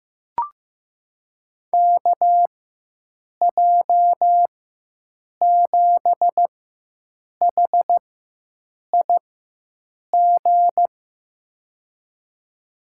Morse code, heard as a keyed tone, about 16 wpm, spaced as for 7 wpm, 700 Hz. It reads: KJ7HIG
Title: USA Call Signs - 6 Characters - 4x Character Spacing - 15wpm